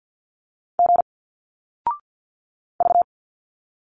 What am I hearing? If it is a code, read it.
Morse code, heard as a keyed tone, about 51 wpm, 700 Hz. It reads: G V
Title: Single Letters 50wpm